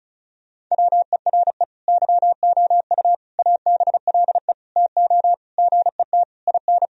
Morse code, heard as a keyed tone, about 34 wpm, 700 Hz. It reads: WERE YOU ABLE TO GET IN
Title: Sentences from Top 1000 Words - Review Part 4 - 35wpm